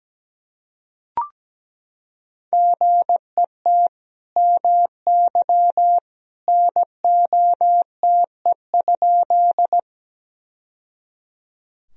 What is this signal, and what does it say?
Morse code, heard as a keyed tone, about 17 wpm, 700 Hz. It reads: GET MY NOTE?
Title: Sentences from Top 1000 Words - Review Part 2 - 17wpm